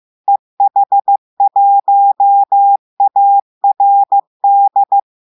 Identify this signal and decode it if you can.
Morse code, heard as a keyed tone, about 15 wpm, 800 Hz. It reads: EH1ARD
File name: eh1ard-amateur-radio-day-2016 mf feed 1